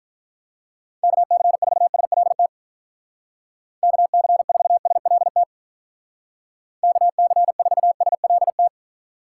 Morse code, heard as a keyed tone, about 43 wpm, 700 Hz. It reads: KC4SLT KC4SLT KC4SLT
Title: Speed Racing - USA 2x3 Call Signs 35wpm